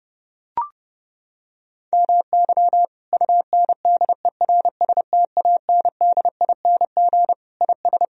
Morse code, heard as a keyed tone, about 30 wpm, 700 Hz. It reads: MY UNDERSTANDING IS
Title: Sentences from Top 1000 Words - Review Part 3 - 30wpm